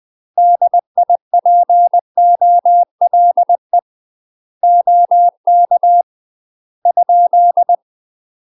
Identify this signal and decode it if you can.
Morse code, heard as a keyed tone, about 20 wpm, 700 Hz. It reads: DIPOLE OK ?
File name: UK MASTER QSOs TRAINING / 20WPM / UK MASTER QSOs 0041